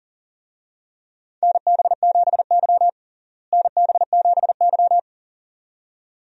Morse code, heard as a keyed tone, about 40 wpm, 700 Hz. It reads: NB7Y NB7Y
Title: USA Call Signs - 4 Characters - Repeated 2x - 40wpm